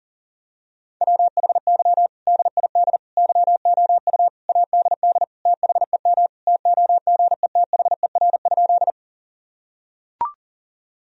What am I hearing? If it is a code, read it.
Morse code, heard as a keyed tone, about 40 wpm, 700 Hz. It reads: WHY DID YOU ADD THEM TOGETHER?
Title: Sentences from Top 1000 Words - Review Part 1 - 40wpm